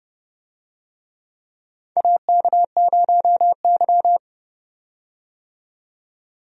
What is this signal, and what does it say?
Morse code, heard as a keyed tone, about 32 wpm, 700 Hz. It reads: AK0Y
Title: USA Call Signs - 4 Characters - 30wpm